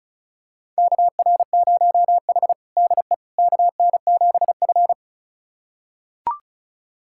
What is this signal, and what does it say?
Morse code, heard as a keyed tone, about 35 wpm, 690 Hz. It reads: KR0H DE KN7F
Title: USA Callsign DE Callsign - 4 Characters - 35wpm